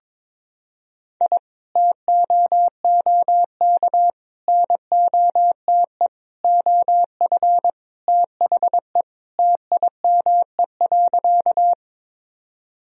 Morse code, heard as a keyed tone, about 22 wpm, 700 Hz. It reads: I TOOK NOTE OF THE TIME.
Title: Sentences from Top 1000 Words - Review Part 2 - 22wpm